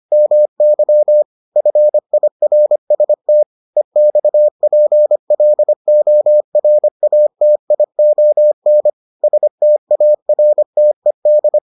Morse code, heard as a keyed tone, about 25 wpm, 600 Hz. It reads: MY FIRST EXPLORATION STARTED